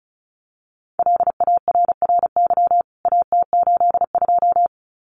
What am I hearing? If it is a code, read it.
Morse code, heard as a keyed tone, about 34 wpm, 700 Hz. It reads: LARRY AT82
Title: CWT Exchange 35wpm